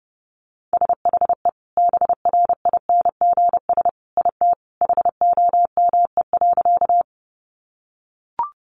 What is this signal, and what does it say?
Morse code, heard as a keyed tone, about 30 wpm, 700 Hz. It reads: SHE BRINGS IT HOME.
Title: Easy Phrases 30wpm